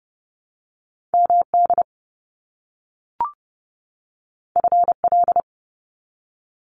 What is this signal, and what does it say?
Morse code, heard as a keyed tone, about 30 wpm, 700 Hz. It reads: MD FL
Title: US States 30wpm